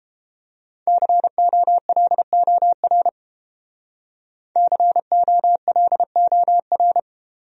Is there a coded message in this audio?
Yes: Morse code from 0.9 to 7.0 s, heard as a keyed tone, about 31 wpm, 700 Hz. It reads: COLOR COLOR